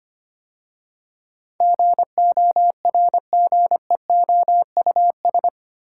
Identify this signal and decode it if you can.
Morse code, heard as a keyed tone, about 25 wpm, 700 Hz. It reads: GORGEOUS
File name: 8 Letter Words - First 3 Letters Given - 25wpm